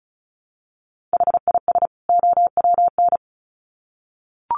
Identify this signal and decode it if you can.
Morse code, heard as a keyed tone, about 34 wpm, 700 Hz. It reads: HIS OWN
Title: Sets of 2 Words 35wpm